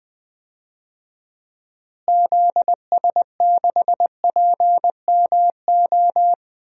Morse code, heard as a keyed tone, about 20 wpm, 700 Hz. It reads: ZS6PMO
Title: DXCC-100 Call Signs - 6 Characters - Repeated 2x - 20wpm